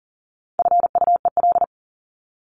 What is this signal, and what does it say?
Morse code, heard as a keyed tone, about 38 wpm, 700 Hz. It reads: FUEL
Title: Top 1751-2000 Words 40wpm